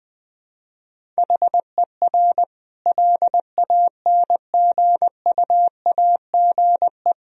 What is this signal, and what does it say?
Morse code, heard as a keyed tone, about 20 wpm, 700 Hz. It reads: HER LANGUAGE
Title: Sentences from Top 1000 Words - Review Part 3 - 20wpm